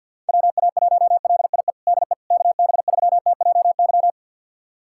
Morse code, heard as a keyed tone, about 51 wpm, 700 Hz. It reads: WA1LIE DE KB3TJY
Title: USA Callsign DE Callsign - 6 Characters - 50wpm